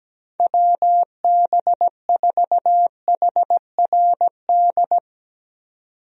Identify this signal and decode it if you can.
Morse code, heard as a keyed tone, about 17 wpm, 700 Hz. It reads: WB4HRD
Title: USA Call Signs - 6 Characters - 17wpm